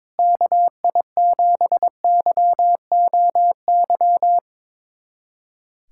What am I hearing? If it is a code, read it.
Morse code, heard as a keyed tone, about 22 wpm, 700 Hz. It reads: KI7YOY
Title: USA Callsign DE Callsign - 6 Characters - 22wpm